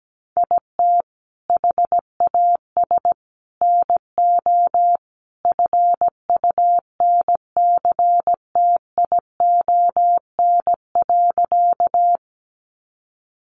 Morse code, heard as a keyed tone, about 17 wpm, 700 Hz. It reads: IT HAS NO FUNCTION.
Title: Sentences from Top 1100 Words 17wpm